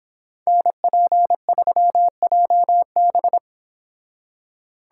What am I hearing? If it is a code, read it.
Morse code, heard as a keyed tone, about 26 wpm, 700 Hz. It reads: NP3JB
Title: Speed Racing - USA Call Signs 22wpm